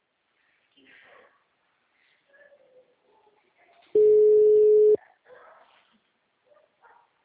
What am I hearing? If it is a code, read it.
Ringback tone (Europe, 425 Hz), with speech.